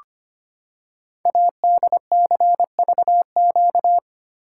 Morse code, heard as a keyed tone, about 25 wpm, 700 Hz. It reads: ADCVQ
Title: Code Groups - 5 Letters - 25wpm